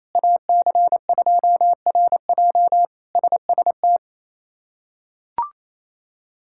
Morse code, heard as a keyed tone, about 28 wpm, 700 Hz. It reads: AC2RJ SST
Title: K1USN SST QSO Practice - 28wpm